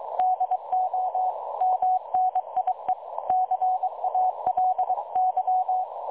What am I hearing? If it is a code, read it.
Morse code, heard as a keyed tone, about 22 wpm, 740 Hz. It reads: DO MNIE CALY
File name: dlazaawansowanych / nagraniazpasma / websdr recording 3565.0kHz